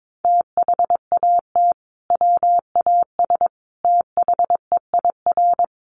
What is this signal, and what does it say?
Morse code, heard as a keyed tone, about 22 wpm, 700 Hz. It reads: THAT WAS THEIR
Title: Sentences from Top 1200 Words 22wpm